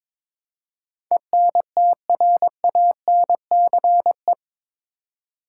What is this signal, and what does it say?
Morse code, heard as a keyed tone, about 22 wpm, 700 Hz. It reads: ENTRANCE
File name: practice / 22wpm / Eight Letter Words - 22wpm